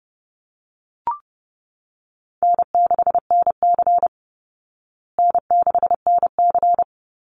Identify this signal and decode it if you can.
Morse code, heard as a keyed tone, about 30 wpm, 700 Hz. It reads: N6NC N6NC
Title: Easy Call Signs Repeated 3x - 30wpm